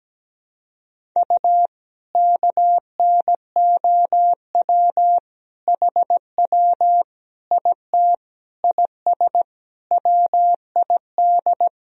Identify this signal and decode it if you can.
Morse code, heard as a keyed tone, about 17 wpm, 700 Hz. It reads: U KNOW HW IT IS WID
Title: Sentences from Top 100 Words and Abbreviations 17wpm